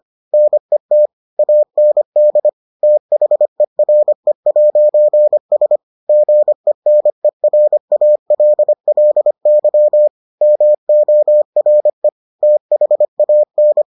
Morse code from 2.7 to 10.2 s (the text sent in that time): THERE'S GENERALLY